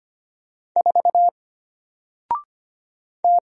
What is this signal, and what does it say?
Morse code, heard as a keyed tone, about 25 wpm, 700 Hz. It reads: 4 T
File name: Single-Letter-Number-25wpm